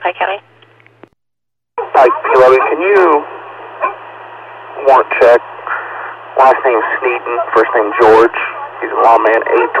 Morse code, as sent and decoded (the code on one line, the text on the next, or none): none